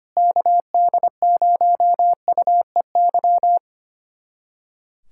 Morse code, heard as a keyed tone, about 25 wpm, 700 Hz. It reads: KD0UEY
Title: USA Callsign DE Callsign 25wpm